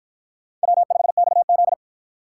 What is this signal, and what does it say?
Morse code, heard as a keyed tone, about 49 wpm, 700 Hz. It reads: W5XZ